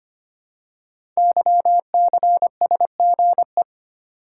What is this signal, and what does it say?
Morse code, heard as a keyed tone, about 25 wpm, 700 Hz. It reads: YCSGE
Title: Code Groups - 5 Letters - 25wpm